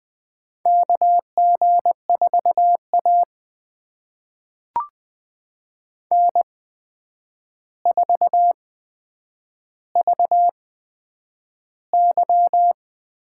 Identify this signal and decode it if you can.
Morse code, heard as a keyed tone, about 21 wpm, spaced as for 9 wpm, 700 Hz. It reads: KG4A N4VY
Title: USA Call Signs - 4 Characters - 8x Character Spacing - 20wpm